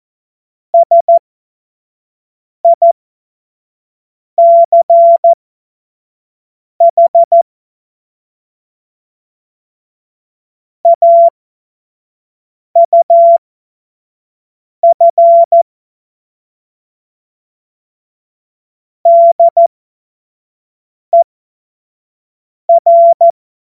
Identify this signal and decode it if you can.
Morse code, heard as a keyed tone, about 14 wpm, spaced as for 5 wpm, 680 Hz. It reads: SICH AUF DER
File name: Morsekurs / Witze / Witz 03. Mai 2022 25 F70